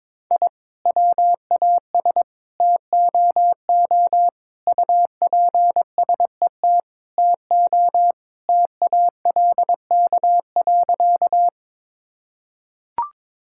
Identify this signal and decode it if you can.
Morse code, heard as a keyed tone, about 22 wpm, 700 Hz. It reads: I WAS TOO UPSET TO TALK.
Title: Sentences from Top 1000 Words - Review Part 5 - 22wpm